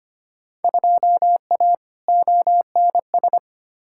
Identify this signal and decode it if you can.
Morse code, heard as a keyed tone, about 25 wpm, 700 Hz. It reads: 2A ONS
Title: Field Day Exchanges Encore - 25wpm